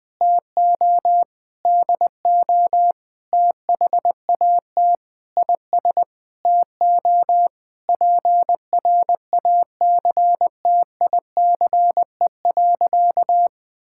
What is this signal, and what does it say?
Morse code, heard as a keyed tone, about 20 wpm, 700 Hz. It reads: TO DO THAT IS TO PRACTICE.
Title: Sentences with 9-Word Phrases - 20wpm